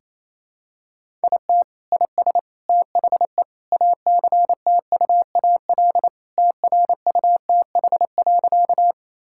Morse code, heard as a keyed tone, about 27 wpm, 700 Hz. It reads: IT IS THE ACTUAL TRUTH.
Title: Sentences from Top 1000 Words - Review Part 5 - 28wpm